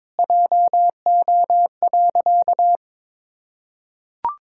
Morse code, heard as a keyed tone, about 22 wpm, 700 Hz. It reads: JO.
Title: Pangram Challenge v2 - 22wpm